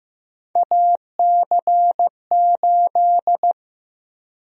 Morse code, heard as a keyed tone, about 15 wpm, 700 Hz. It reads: AC8
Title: USA Call Sign Prefixes - Repeated 2x - 15wpm